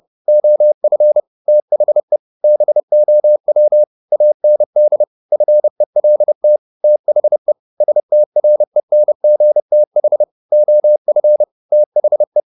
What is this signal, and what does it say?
Morse code, heard as a keyed tone, about 30 wpm, 600 Hz. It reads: OF THE BOW AND FELT THE STRENGTH OF THE
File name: the old man and the sea / the old man and the sea 1 30wpm